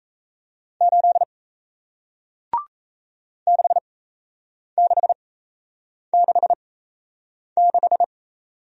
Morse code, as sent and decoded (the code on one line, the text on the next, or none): ---.. / -.... -.... -.... -....
8 6666